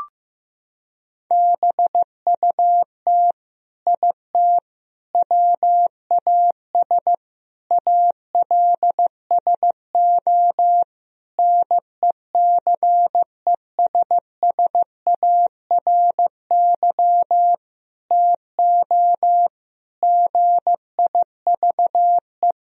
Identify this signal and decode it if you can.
Morse code, heard as a keyed tone, about 16 wpm, 700 Hz. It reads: BUT IT WAS ALSO NECESSARY TO GIVE
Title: Sentences from Top 1000 Words - Review Part 2 - 15wpm